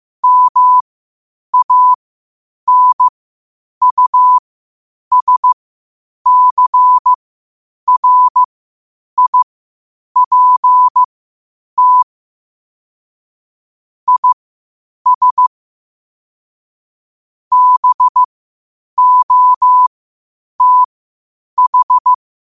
Morse code, heard as a keyed tone, about 15 wpm, spaced as for 8 wpm, 1000 Hz. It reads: MANUSCRIPT IS BOTH
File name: qotdfiles / 8wpm